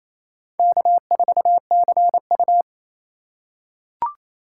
Morse code, heard as a keyed tone, about 27 wpm, 700 Hz. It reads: K4CU
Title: USA Call Signs - 4x Character Spacing - 28wpm